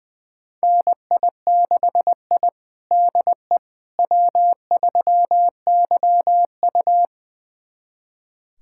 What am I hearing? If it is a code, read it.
Morse code, heard as a keyed tone, about 20 wpm, 700 Hz. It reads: NI6I DE W3YU
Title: USA Callsign DE Callsign - 4 Characters - 20wpm